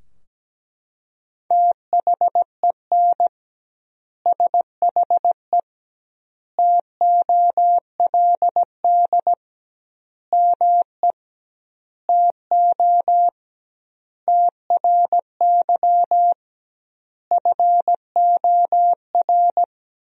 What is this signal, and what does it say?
Morse code, heard as a keyed tone, about 17 wpm, 700 Hz. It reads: THEN SHE TOLD ME TO TRY FOR
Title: 2x Word-Spacing - Sentences from Top 400 Words - 17wpm